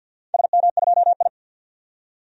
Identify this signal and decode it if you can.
Morse code, heard as a keyed tone, about 47 wpm, 700 Hz. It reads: SM2I
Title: DXCC-100 Call Signs - 4 Characters - 50wpm